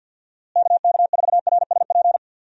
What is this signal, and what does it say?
Morse code, heard as a keyed tone, about 46 wpm, 700 Hz. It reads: KK4RSP
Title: USA Call Signs - 6 Characters - Repeated 2x - 50wpm